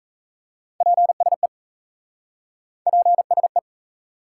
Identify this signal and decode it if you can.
Morse code, heard as a keyed tone, about 37 wpm, 700 Hz. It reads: PSE PSE